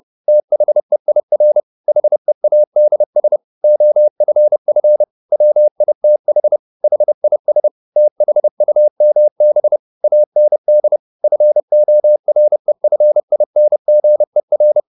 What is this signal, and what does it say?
Morse code, heard as a keyed tone, about 30 wpm, 600 Hz. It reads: THEIR HEADS OFF WITH HIS THUMB AND FOREFINGER